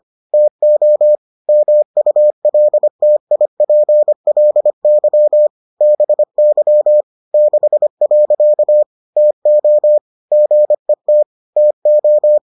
Morse code, heard as a keyed tone, about 25 wpm, 600 Hz. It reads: TO MULTIPLY BY 6. TO GET TO